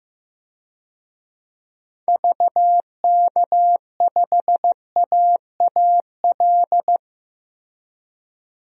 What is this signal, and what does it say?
Morse code, heard as a keyed tone, about 15 wpm, 700 Hz. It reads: VK5AAL
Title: DXCC-100 Call Signs - 6 Characters - Repeated 2x - 15wpm